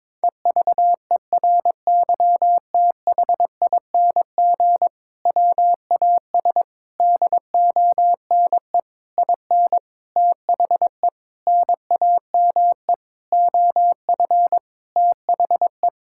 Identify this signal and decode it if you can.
Morse code, heard as a keyed tone, about 22 wpm, 700 Hz. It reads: EVERYTHING WAS DONE IN THE NAME OF THE
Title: Sentences from Top 1600 Words 22wpm